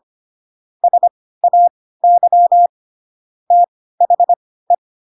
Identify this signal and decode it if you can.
Morse code, heard as a keyed tone, about 25 wpm, spaced as for 16 wpm, 700 Hz. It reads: SAY THE
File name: Pinocchio / Pinocchio ch1 10 25 ik1hge